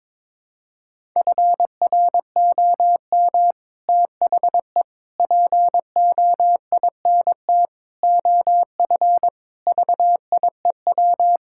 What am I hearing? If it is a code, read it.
Morse code, heard as a keyed tone, about 22 wpm, 700 Hz. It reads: FROM THE POINT OF VIEW
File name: practice / 22wpm / Sets of 6 Words - 22wpm